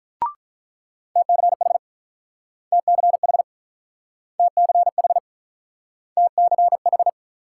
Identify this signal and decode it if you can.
Morse code, heard as a keyed tone, about 46 wpm, 700 Hz. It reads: TCH TCH TCH TCH